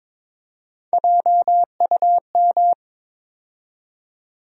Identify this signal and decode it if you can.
Morse code, heard as a keyed tone, about 22 wpm, 700 Hz. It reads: JUM
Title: USA Call Sign Postfixes - 3 Letters 22wpm